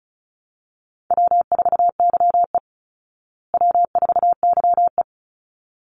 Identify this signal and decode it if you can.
Morse code, heard as a keyed tone, about 34 wpm, 700 Hz. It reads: W4YE W4YE
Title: Easy Call Signs Repeated 2x - 35wpm